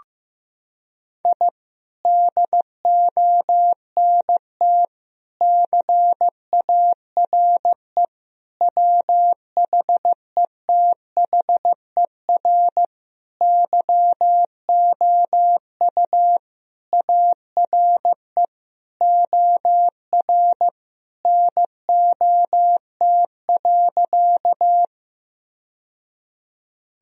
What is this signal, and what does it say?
Morse code, heard as a keyed tone, about 15 wpm, 700 Hz. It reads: I DONT CARE WHETHER YOU ARE OR NOT.
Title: Sentences from Top 1000 Words - Review Part 3 - 15wpm